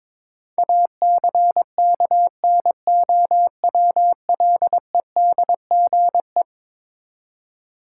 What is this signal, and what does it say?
Morse code, heard as a keyed tone, about 22 wpm, 700 Hz. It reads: ACKNOWLEDGE
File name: practice / 22wpm / Eleven Letter Words - 22wpm